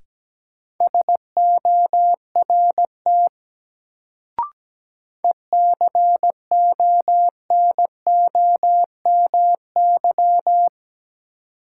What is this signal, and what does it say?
Morse code, heard as a keyed tone, about 17 wpm, 700 Hz. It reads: SORT ECONOMY
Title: Sentences from Top 1000 Words - Review Part 3 - New Word Drill - 17wpm